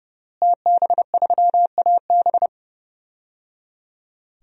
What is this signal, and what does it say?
Morse code, heard as a keyed tone, about 30 wpm, 700 Hz. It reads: TB3AB
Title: DXCC-100 Call Signs - 5 Characters - Repeated 3x - 30wpm